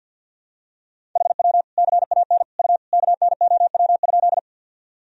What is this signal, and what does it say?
Morse code, heard as a keyed tone, about 46 wpm, 700 Hz. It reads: HW CAN U KNOW?
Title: Sentences from Top 100 Words and Abbreviations 50wpm